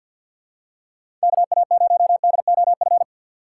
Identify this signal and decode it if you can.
Morse code, heard as a keyed tone, about 53 wpm, 700 Hz. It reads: KA0DOF
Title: USA Call Signs - 6 Characters - 3x Character Spacing - 50wpm